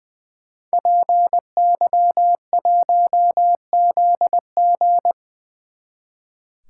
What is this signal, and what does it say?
Morse code, heard as a keyed tone, about 22 wpm, 700 Hz. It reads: PY1ZG
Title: Speed Racing - USA 2x2 Call Signs 20wpm